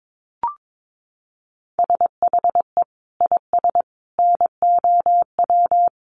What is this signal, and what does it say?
Morse code, heard as a keyed tone, about 22 wpm, 700 Hz. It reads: SHE IS NOW
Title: Sentences from Top 1100 Words 22wpm